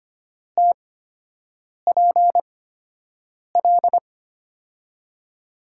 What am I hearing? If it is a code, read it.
Morse code, heard as a keyed tone, about 25 wpm, spaced as for 7 wpm, 700 Hz. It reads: TPL
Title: USA Call Signs - 5 Characters - 8x Character Spacing - 25wpm